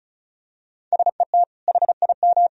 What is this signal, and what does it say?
Morse code, heard as a keyed tone, about 34 wpm, 700 Hz. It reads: SET HIM
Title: Sentences from Top 1000 Words - Review Part 1 - 35wpm